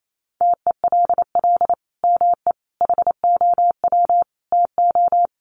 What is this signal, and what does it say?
Morse code, heard as a keyed tone, about 28 wpm, 700 Hz. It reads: TELL ME HOW TO
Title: Sentences from Top 1500 Words 28wpm